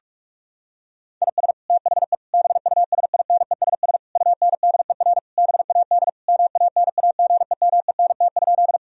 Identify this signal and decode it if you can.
Morse code, heard as a keyed tone, about 42 wpm, 700 Hz. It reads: IS THE BUSINESS UNDER BAD MANAGEMENT?
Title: Sentences from Top 1000 Words - Review Part 4 - 45wpm